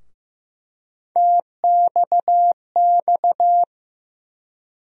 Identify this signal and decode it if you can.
Morse code, heard as a keyed tone, about 15 wpm, 700 Hz. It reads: TXX
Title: USA Call Sign Postfixes - 3 Letters 15wpm